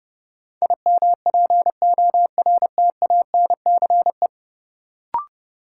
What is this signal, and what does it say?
Morse code, heard as a keyed tone, about 32 wpm, 700 Hz. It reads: IMPORTANCE
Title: Sets of 3 Words - New Word Drill - 30wpm